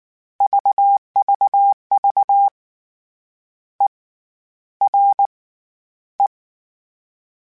Morse code, heard as a keyed tone, about 19 wpm, 800 Hz. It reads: VVV E R E
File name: Short Runs / Short Runs-mp3s / ERE @18